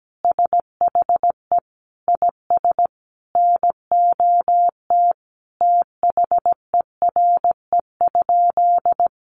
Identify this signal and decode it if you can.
Morse code, heard as a keyed tone, about 17 wpm, 700 Hz. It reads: SHE IS NOT THERE?
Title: Sentences from Top 600 Words 17wpm